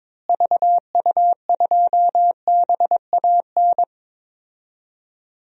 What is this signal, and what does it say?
Morse code, heard as a keyed tone, about 22 wpm, 700 Hz. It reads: VU2BAN
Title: DXCC-100 Call Signs - 6 Characters - Repeated 3x - 22wpm